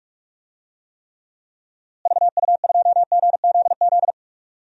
Morse code, heard as a keyed tone, about 43 wpm, 700 Hz. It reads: VU2GZZ
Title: DXCC-100 Call Signs - 6 Characters - Repeated 3x - 45wpm